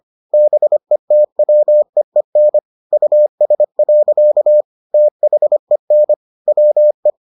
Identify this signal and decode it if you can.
Morse code, heard as a keyed tone, about 25 wpm, 600 Hz. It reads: BETWEEN US. THEN WE